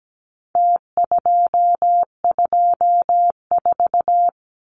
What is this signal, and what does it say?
Morse code, heard as a keyed tone, about 17 wpm, 700 Hz. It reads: T224
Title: Numbers - Contest Style - 4-Digit 17wpm